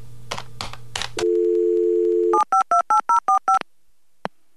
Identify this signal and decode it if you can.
DTMF, heard as telephone touch tones, tones of about 120 ms. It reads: 7528045